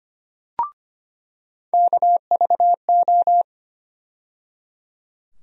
Morse code, heard as a keyed tone, about 25 wpm, 700 Hz. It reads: KVO
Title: USA Call Sign Postfixes - 3 Letters 25wpm